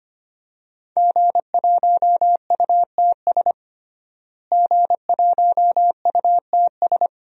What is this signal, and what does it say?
Morse code, heard as a keyed tone, about 25 wpm, 700 Hz. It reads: G1UTS G1UTS
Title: DXCC-100 Call Signs - 5 Characters - Repeated 2x - 25wpm